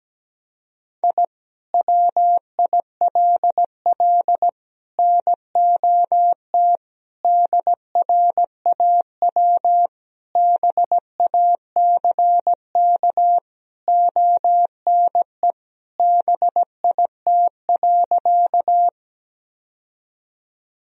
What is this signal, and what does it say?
Morse code, heard as a keyed tone, about 17 wpm, 700 Hz. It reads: I WILL NOT DRAW BACK ONE BIT.
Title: Sentences from Top 1000 Words - Review Part 1 - 17wpm